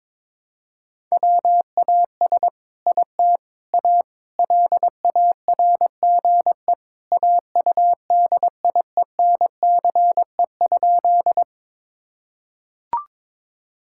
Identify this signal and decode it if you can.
Morse code, heard as a keyed tone, about 22 wpm, 700 Hz. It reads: WAS IT A LARGE AUDIENCE?
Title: Sentences from Top 1000 Words - Review Part 5 - 22wpm